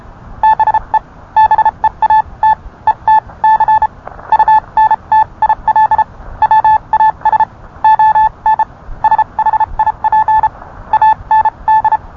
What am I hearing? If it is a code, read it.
Morse code, heard as a keyed tone, about 31 wpm, spaced as for 26 wpm, 870 Hz. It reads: BE BEAT ETC UNTIL WAS ON SHIP AND